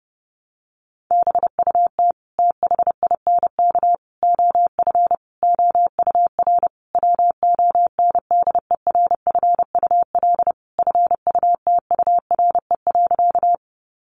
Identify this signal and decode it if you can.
Morse code, heard as a keyed tone, about 30 wpm, 700 Hz. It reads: BUT THINK OF OUR WONDERFUL FUTURE.
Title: Sentences from Top 1100 Words 30wpm